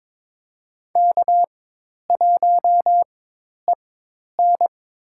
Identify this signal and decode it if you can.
Morse code, heard as a keyed tone, about 22 wpm, spaced as for 10 wpm, 700 Hz. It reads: K1EN
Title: USA Call Signs - 4x Character Spacing - 22wpm